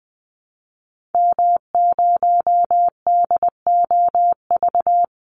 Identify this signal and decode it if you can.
Morse code, heard as a keyed tone, about 20 wpm, 700 Hz. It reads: M0DOV
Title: Call Signs Repeated 2x - 20wpm